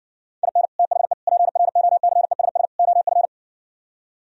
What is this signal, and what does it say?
Morse code, heard as a keyed tone, about 70 wpm, 700 Hz. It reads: IN THE PROCESS OF